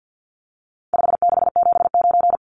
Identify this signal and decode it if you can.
Morse code, heard as a keyed tone, about 50 wpm, 700 Hz. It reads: 5679
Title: Numbers - 4-Digit 50wpm